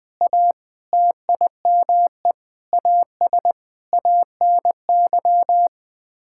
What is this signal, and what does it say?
Morse code, heard as a keyed tone, about 21 wpm, 700 Hz. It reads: A TIME AS ANY
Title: Sets of 8 Words - 20wpm